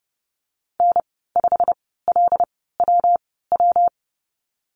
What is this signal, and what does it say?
Morse code, heard as a keyed tone, about 30 wpm, spaced as for 17 wpm, 700 Hz. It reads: N5LWW
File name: Lesson 19g - Letter w - Call Signs 30wpm